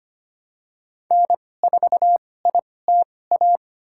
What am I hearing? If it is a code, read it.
Morse code, heard as a keyed tone, about 25 wpm, spaced as for 18 wpm, 700 Hz. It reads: N4ITA
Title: USA Call Signs - 5 Characters - 2x Character Spacing - 25wpm